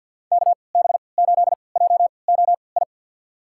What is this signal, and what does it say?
Morse code, heard as a keyed tone, about 46 wpm, spaced as for 28 wpm, 700 Hz. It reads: KB8JOI
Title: USA Call Signs - 6 Characters - 3x Character Spacing - 50wpm